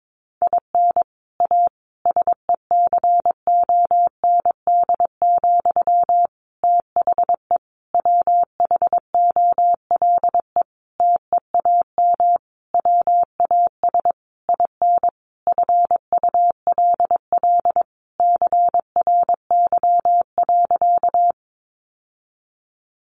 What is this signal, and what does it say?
Morse code, heard as a keyed tone, about 22 wpm, 700 Hz. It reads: IN A SECOND, THE WHOLE TEAM WAS IN FULL CRY.
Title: Sentences from Top 1500 Words 22wpm